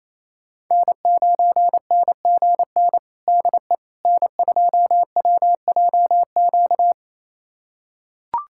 Morse code, heard as a keyed tone, about 28 wpm, 700 Hz. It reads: N9NGN DE N2WJQ
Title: USA Callsign DE Callsign 28wpm